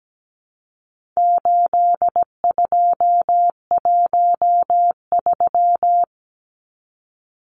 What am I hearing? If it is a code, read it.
Morse code, heard as a keyed tone, about 17 wpm, 700 Hz. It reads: 8213
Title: Numbers - 4-Digit 17wpm